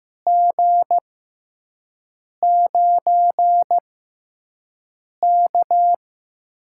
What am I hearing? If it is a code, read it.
Morse code, heard as a keyed tone, about 15 wpm, 700 Hz. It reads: G 9 K
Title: USA Call Signs - 6x Character Spacing - 15wpm